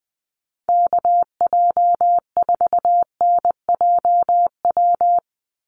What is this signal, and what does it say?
Morse code, heard as a keyed tone, about 20 wpm, 700 Hz. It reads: KJ4NJW
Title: USA Call Signs 20wpm